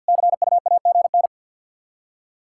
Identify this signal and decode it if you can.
Morse code, heard as a keyed tone, about 53 wpm, 700 Hz. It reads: CUAGN
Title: Practice QSO - Exchange 4 - Ending - 50wpm